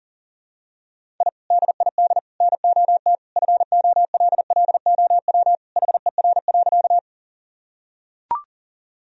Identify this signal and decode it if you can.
Morse code, heard as a keyed tone, about 40 wpm, 700 Hz. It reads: I DID NOT FOLLOW HER.
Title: Sentences from Top 1000 Words - Review Part 1 - 40wpm